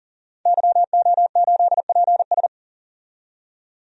Morse code, heard as a keyed tone, about 40 wpm, 700 Hz. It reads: YO8PS
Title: DXCC-100 Call Signs - 5 Characters - 40wpm